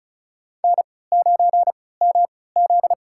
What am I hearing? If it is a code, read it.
Morse code, heard as a keyed tone, about 34 wpm, spaced as for 20 wpm, 700 Hz. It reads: N9MZ
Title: USA Call Signs - 3x Character Spacing - 35wpm